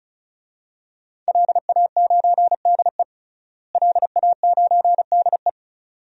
Morse code, heard as a keyed tone, about 34 wpm, 700 Hz. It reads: LA9DE LA9DE
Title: DXCC-100 Call Signs - 5 Characters - Repeated 2x - 35wpm